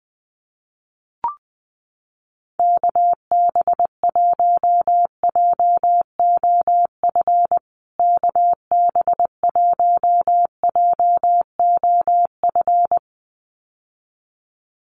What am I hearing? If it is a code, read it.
Morse code, heard as a keyed tone, about 20 wpm, 700 Hz. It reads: KB1JOF KB1JOF
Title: USA Call Signs Repeated 2x 20wpm